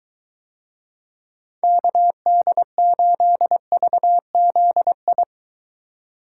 Morse code, heard as a keyed tone, about 23 wpm, 700 Hz. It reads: KD8VZI